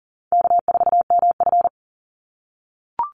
Morse code, heard as a keyed tone, about 40 wpm, 700 Hz. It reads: K4MF
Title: NA SOTA Chaser Callsigns 40wpm